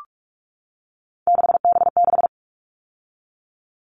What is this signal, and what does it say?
Morse code, heard as a keyed tone, about 44 wpm, 700 Hz. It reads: 6B6
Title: All Cut Numbers - 3-Digit 45wpm